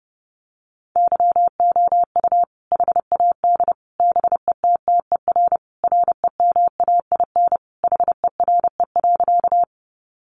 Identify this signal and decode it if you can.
Morse code, heard as a keyed tone, about 30 wpm, 700 Hz. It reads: YOU HAD BETTER REMAIN HERE.
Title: Sentences from Top 1200 Words 30wpm